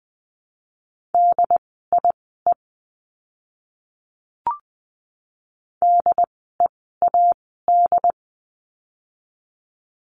Morse code, heard as a keyed tone, about 20 wpm, spaced as for 14 wpm, 700 Hz. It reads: DIE DEAD
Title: 075 Lesson 13f - Letter L - Words 20wpm